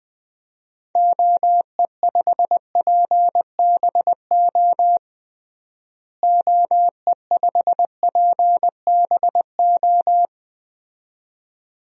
Morse code, heard as a keyed tone, about 21 wpm, 700 Hz. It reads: OE5PBO OE5PBO
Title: DXCC-100 Call Signs - 6 Characters - Repeated 2x - 20wpm